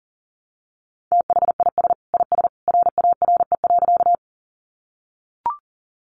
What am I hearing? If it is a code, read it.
Morse code, heard as a keyed tone, about 40 wpm, 700 Hz. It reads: THIS IS RARE.
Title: Sentences from Top 1100 Words 40wpm